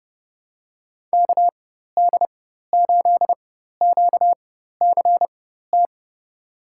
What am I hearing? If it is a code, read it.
Morse code, heard as a keyed tone, about 32 wpm, spaced as for 14 wpm, 700 Hz. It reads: KD8QCT
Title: USA Call Signs - 4x Character Spacing - 30wpm